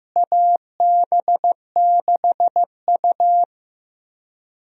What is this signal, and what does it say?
Morse code, heard as a keyed tone, about 16 wpm, 700 Hz. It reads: AB6U
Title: USA Call Signs - 4 Characters - Repeated 3x - 15wpm